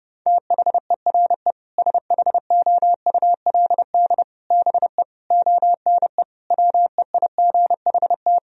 Morse code, heard as a keyed tone, about 30 wpm, 700 Hz. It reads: THERE SHOULD BE ONE WEIGHT
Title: Sentences from Top 1000 Words - Review Part 3 - 30wpm